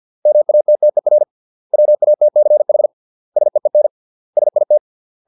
Morse code, heard as a keyed tone, about 50 wpm, 600 Hz. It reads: MATTER WATCH SEEN SIT